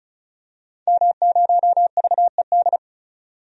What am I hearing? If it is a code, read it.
Morse code, heard as a keyed tone, about 36 wpm, 700 Hz. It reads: M0VED
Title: DXCC-100 Call Signs - 5 Characters - 35wpm